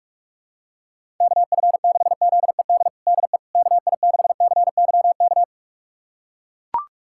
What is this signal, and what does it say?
Morse code, heard as a keyed tone, about 41 wpm, 700 Hz. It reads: KF6ZED DE KI6CYK
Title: USA Callsign DE Callsign - 6 Characters - 45wpm